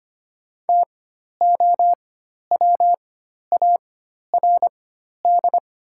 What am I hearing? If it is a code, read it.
Morse code, heard as a keyed tone, about 25 wpm, spaced as for 12 wpm, 700 Hz. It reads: TOWARD
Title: 4x Character Spacing - Top 500 Words - 25wpm